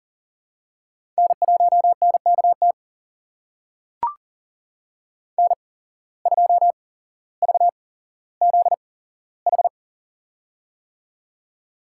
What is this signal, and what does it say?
Morse code, heard as a keyed tone, about 39 wpm, spaced as for 17 wpm, 700 Hz. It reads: N1NKT N2VZH
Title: USA Call Signs - 5 Characters - 8x Character Spacing - 40wpm